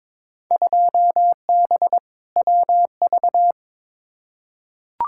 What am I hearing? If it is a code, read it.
Morse code, heard as a keyed tone, about 22 wpm, 700 Hz. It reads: 2B WV
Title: Field Day Exchanges Encore - 22wpm